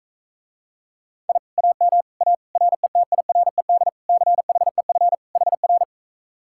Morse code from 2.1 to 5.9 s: A RETIRED CHEF HR

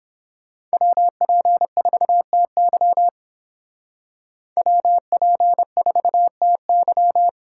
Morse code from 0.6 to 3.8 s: WP4TY

